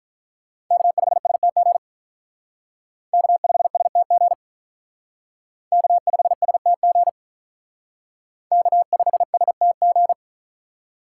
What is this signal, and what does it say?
Morse code, heard as a keyed tone, about 43 wpm, 700 Hz. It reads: K5STG K5STG K5STG K5STG
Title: Speed Racing - USA 1x3 Call Signs 35wpm